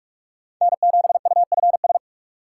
Morse code, heard as a keyed tone, about 44 wpm, 700 Hz. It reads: N7UFS
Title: USA Call Signs - 5 Characters - 4x Character Spacing - 45wpm